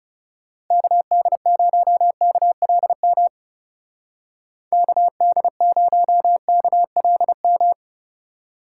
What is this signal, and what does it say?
Morse code, heard as a keyed tone, about 36 wpm, 700 Hz. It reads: KD0KLM KD0KLM
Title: Speed Racing - USA 2x3 Call Signs 30wpm